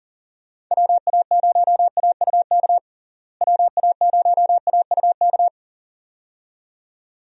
Morse code, heard as a keyed tone, about 40 wpm, 700 Hz. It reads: WA0AUK WA0AUK
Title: USA Call Signs - 6 Characters - Repeated 2x - 40wpm